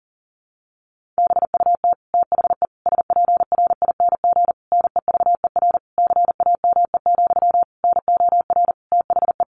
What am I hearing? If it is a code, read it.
Morse code, heard as a keyed tone, about 37 wpm, 700 Hz. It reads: BUT THE SPRING NEVER CAME, NOR THE
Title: Sentences from Top 1500 Words 40wpm